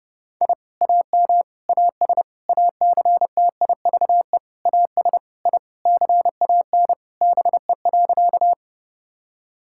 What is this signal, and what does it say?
Morse code, heard as a keyed tone, about 30 wpm, 700 Hz. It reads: I AM AS ACTIVE AS I CAN BE.
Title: Sentences from Top 1000 Words - Review Part 6 - 30wpm